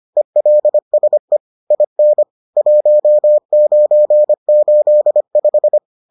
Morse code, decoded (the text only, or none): ELSE IN 1985